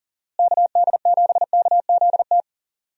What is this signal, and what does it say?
Morse code, heard as a keyed tone, about 40 wpm, 700 Hz. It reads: KD7KZT
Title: USA Call Signs - 6 Characters - Repeated 2x - 40wpm